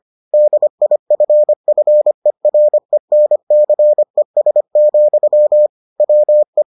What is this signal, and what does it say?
Morse code, heard as a keyed tone, about 25 wpm, 600 Hz. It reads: DIFFERENCES, WE